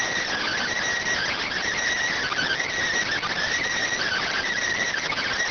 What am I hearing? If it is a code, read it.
Emergency Alert System (SAME) tones (header bursts), heard as harsh buzzing bursts. It is heard from 0.3 to 4.5 s.